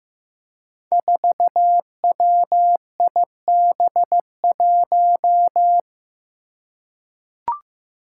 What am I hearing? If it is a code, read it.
Morse code, heard as a keyed tone, about 15 wpm, 700 Hz. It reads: 4WIB1
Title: Five Challenging Letter-Number Combinations 15wpm